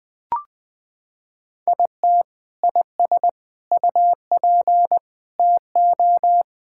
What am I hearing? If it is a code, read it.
Morse code, heard as a keyed tone, about 20 wpm, 700 Hz. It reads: IT IS UP TO
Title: Sentences with 3-Word Phrases - 20wpm